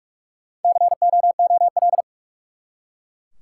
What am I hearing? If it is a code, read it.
Morse code, heard as a keyed tone, about 42 wpm, 700 Hz. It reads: COOL
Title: Four Letter Words - 45wpm